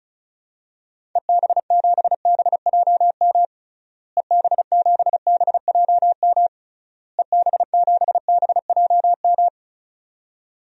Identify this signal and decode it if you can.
Morse code, heard as a keyed tone, about 34 wpm, 700 Hz. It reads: EB7BJM EB7BJM EB7BJM
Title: DXCC-100 Call Signs - 6 Characters - Repeated 3x - 35wpm